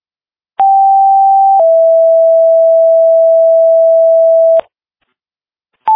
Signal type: two-tone page